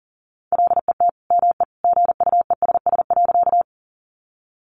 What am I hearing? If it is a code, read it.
Morse code, heard as a keyed tone, about 40 wpm, 700 Hz. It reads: LET ME GUESS.
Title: Sentences from Top 1500 Words 40wpm